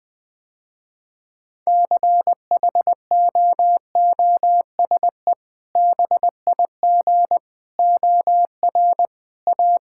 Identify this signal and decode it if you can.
Morse code, heard as a keyed tone, about 20 wpm, 700 Hz. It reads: CHOOSE BIG OR A
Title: Sentences with 2-Word Phrases - Spoken Beginning - 20wpm